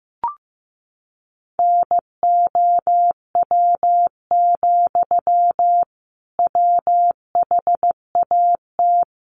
Morse code, heard as a keyed tone, about 15 wpm, 700 Hz. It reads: NOW, WHAT
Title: Sentences from Top 200 Words 15wpm